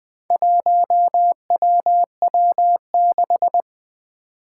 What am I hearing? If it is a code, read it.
Morse code, heard as a keyed tone, about 20 wpm, 700 Hz. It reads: 1WW6
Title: Four Challenging Letter-Number Combinations 20wpm